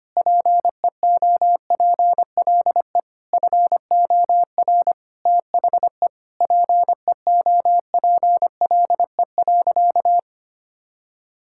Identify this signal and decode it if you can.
Morse code, heard as a keyed tone, about 25 wpm, 700 Hz. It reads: PEOPLE FOR THE PEOPLE.
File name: practice / 25wpm / Sentences with 9-Word Phrases - 25wpm